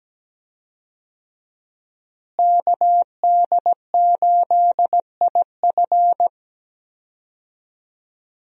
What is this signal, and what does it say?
Morse code, heard as a keyed tone, about 17 wpm, 700 Hz. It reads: KD8IF
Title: USA Call Signs - 5 Characters - Repeated 2x - 17wpm